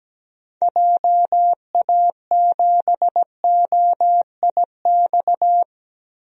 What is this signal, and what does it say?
Morse code, heard as a keyed tone, about 17 wpm, 700 Hz. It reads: JA7OIX
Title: DXCC-100 Call Signs - 6 Characters - 17wpm